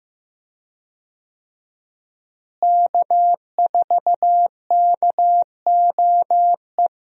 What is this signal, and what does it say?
Morse code, heard as a keyed tone, about 15 wpm, 700 Hz. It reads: K4KOE
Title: USA Call Signs - 5 Characters - Repeated 3x - 15wpm